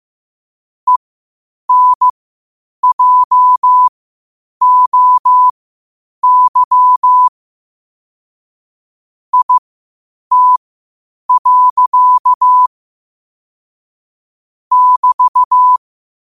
Morse code, heard as a keyed tone, about 15 wpm, spaced as for 8 wpm, 1000 Hz. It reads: ENJOY IT. <BT>
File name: qotdfiles / 8wpm